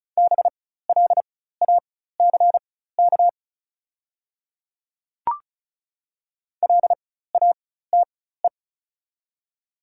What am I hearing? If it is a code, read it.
Morse code, heard as a keyed tone, about 36 wpm, spaced as for 16 wpm, 690 Hz. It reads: BLACK LATE
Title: 4x Character Spacing - Top 500 Words - 35wpm